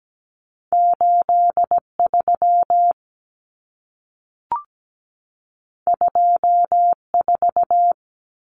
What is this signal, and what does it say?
Morse code, heard as a keyed tone, about 17 wpm, 700 Hz. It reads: 83 24
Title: Numbers - 2-Digit 17wpm